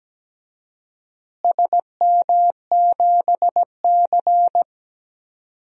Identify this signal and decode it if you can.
Morse code, heard as a keyed tone, about 17 wpm, 700 Hz. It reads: SM7C
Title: DXCC-100 Call Signs - 4 Characters - Repeated 3x - 17wpm